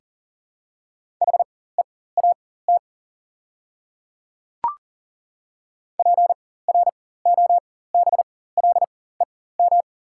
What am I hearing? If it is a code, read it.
Morse code, heard as a keyed tone, about 40 wpm, spaced as for 19 wpm, 690 Hz. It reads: HEAT PROBLEM
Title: 4x Character Spacing - Top 500 Words - 40wpm